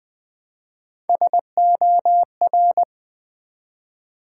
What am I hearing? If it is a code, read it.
Morse code, heard as a keyed tone, about 20 wpm, 700 Hz. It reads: SOR
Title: Build A Word - 1 Letter At A Time - 5-Letter Words - 20wpm